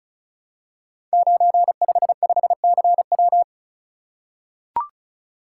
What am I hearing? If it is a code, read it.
Morse code, heard as a keyed tone, about 36 wpm, 690 Hz. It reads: 955CW
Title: Code Groups - 5 Letter:Numbers - 35wpm